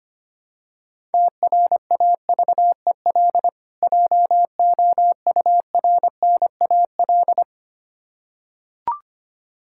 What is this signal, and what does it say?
Morse code, heard as a keyed tone, about 25 wpm, 700 Hz. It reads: TRAVEL JOURNAL
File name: practice / 25wpm / Difficult Words with Hints - 25wpm